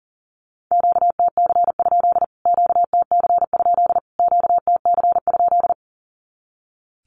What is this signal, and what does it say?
Morse code, heard as a keyed tone, about 40 wpm, 700 Hz. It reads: QTC? QTC? QTC?
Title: Q-Codes Repeated 3x 40wpm